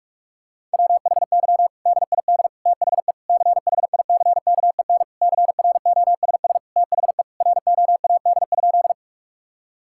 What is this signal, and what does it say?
Morse code, heard as a keyed tone, about 42 wpm, 700 Hz. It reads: WHY DID THE CHICKEN CROSS THE ROAD?
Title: Sentences from Top 1000 Words - Review Part 5 - 45wpm